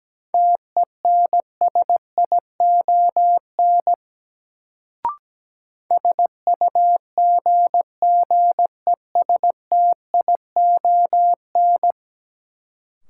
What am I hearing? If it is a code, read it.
Morse code, heard as a keyed tone, about 17 wpm, 700 Hz. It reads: TENSION SUGGESTION
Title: Sentences from Top 1600 Words - New Word Drill Alt - 17wpm